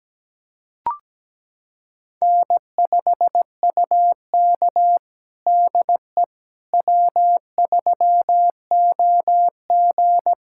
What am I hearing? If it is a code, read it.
Morse code, heard as a keyed tone, about 17 wpm, 700 Hz. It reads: N5UK DE W3OG
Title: USA Callsign DE Callsign - 4 Characters - 17wpm